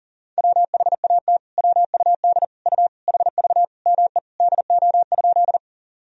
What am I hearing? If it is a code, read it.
Morse code, heard as a keyed tone, about 37 wpm, 700 Hz. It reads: WHAT WUD U HV ME DO?
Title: Sentences from Top 200 Words and Abbreviations 40wpm